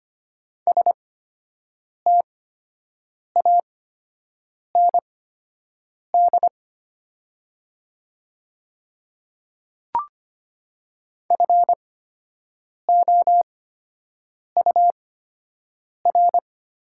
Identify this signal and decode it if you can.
Morse code, heard as a keyed tone, about 25 wpm, spaced as for 7 wpm, 700 Hz. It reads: STAND FOUR
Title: 8x Character Spacing - Top 300 Words - 25wpm